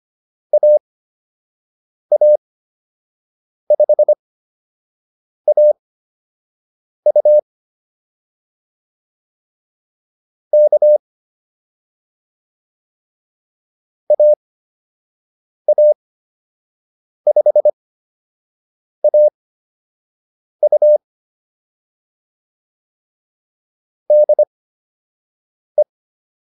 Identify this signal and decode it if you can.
Morse code, heard as a keyed tone, about 25 wpm, spaced as for 6 wpm, 600 Hz. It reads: AA5AU K AA5AU DE